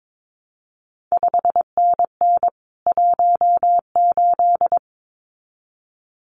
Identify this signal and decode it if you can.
Morse code, heard as a keyed tone, about 22 wpm, 700 Hz. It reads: 5NN 18
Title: CQ WW DX Contest Exchange Encore - 22wpm